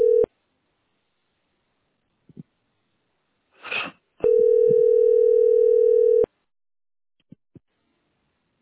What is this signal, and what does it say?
Ringback tone (North America), 440 + 480 Hz.